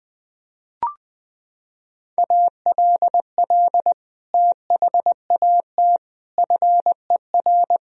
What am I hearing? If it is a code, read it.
Morse code, heard as a keyed tone, about 20 wpm, 700 Hz. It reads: ALL THAT FER
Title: Sentences from Top 500 Words and Abbreviations 20wpm